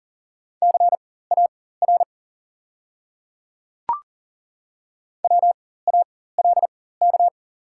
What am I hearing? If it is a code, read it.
Morse code, heard as a keyed tone, about 42 wpm, spaced as for 19 wpm, 690 Hz. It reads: CAR WALK
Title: 4x Character Spacing - Top 300 Words - 40wpm